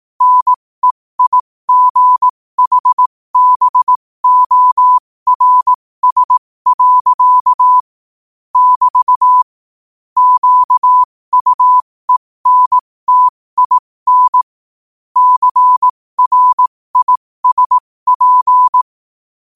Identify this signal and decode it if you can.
Morse code, heard as a keyed tone, about 18 wpm, spaced as for 15 wpm, 1000 Hz. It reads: NEIGHBORS. <BT> QUENTIN CRISP
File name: qotdfiles / 15wpm